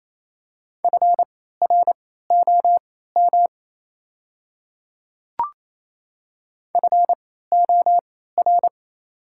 Morse code, heard as a keyed tone, about 28 wpm, spaced as for 16 wpm, 690 Hz. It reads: FROM FOR
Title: 3x Character Spacing - Top 100 Words - 28wpm